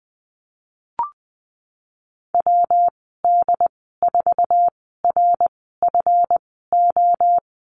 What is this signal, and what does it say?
Morse code, heard as a keyed tone, about 20 wpm, spaced as for 15 wpm, 700 Hz. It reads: WD4RFO
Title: 154 Lesson 25g - Letter p - Call Signs 20wpm